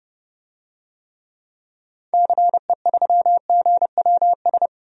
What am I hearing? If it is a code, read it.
Morse code, heard as a keyed tone, about 30 wpm, 700 Hz. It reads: CE3GWS
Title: DXCC-100 Call Signs - 6 Characters - 30wpm